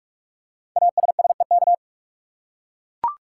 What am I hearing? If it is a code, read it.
Morse code, heard as a keyed tone, about 44 wpm, 700 Hz. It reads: ASSEK
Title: Code Groups - 5 Letters - 45wpm